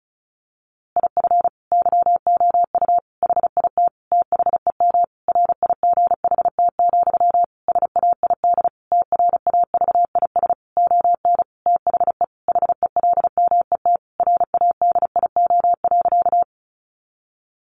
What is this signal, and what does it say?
Morse code, heard as a keyed tone, about 34 wpm, 700 Hz. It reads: IF YOU HIT THEM RIGHT, SAID TRAVIS ON THE HELMET RADIO.
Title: A Sound Of Thunder 35wpm